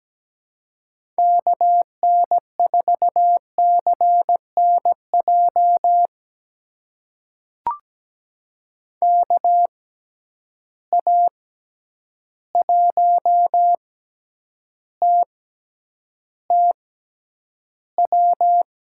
Morse code, heard as a keyed tone, about 17 wpm, spaced as for 9 wpm, 700 Hz. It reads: KN4CNJ KA1TTW
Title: USA Call Signs - 6 Characters - 6x Character Spacing - 17wpm